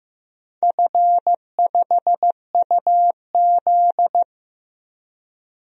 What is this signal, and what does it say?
Morse code, heard as a keyed tone, about 15 wpm, 700 Hz. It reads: F5UZ
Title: DXCC-100 Call Signs - 4 Characters - 15wpm